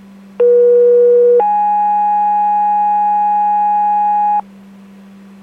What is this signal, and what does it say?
Two-tone page, heard as a one-second tone, then a three-second tone, about 480 then 830 Hz, with noise. It is heard from 0.4 to 4.4 s.